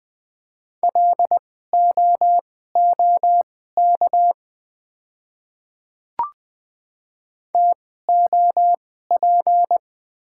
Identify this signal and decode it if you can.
Morse code, heard as a keyed tone, about 20 wpm, spaced as for 14 wpm, 700 Hz. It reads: LOOK TOP
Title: 2x Character Spacing - Top 500 Words - 20wpm